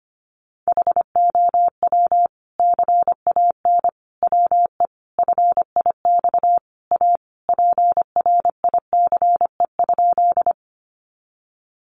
Morse code, heard as a keyed tone, about 25 wpm, 700 Hz. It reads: HOW CAN WE FIX A PRICE?
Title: Sentences from Top 1400 Words 25wpm